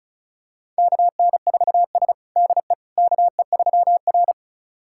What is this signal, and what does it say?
Morse code, heard as a keyed tone, about 36 wpm, 700 Hz. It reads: KN4S DE KE3R
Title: USA Callsign DE Callsign - 4 Characters - 35wpm